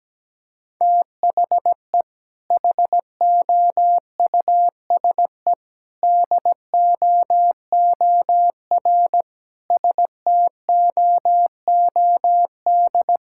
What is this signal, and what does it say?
Morse code, heard as a keyed tone, about 17 wpm, 700 Hz. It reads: THE HOUSE DOOR STOOD
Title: Sentences from Top 1000 Words - Review Part 6 - 17wpm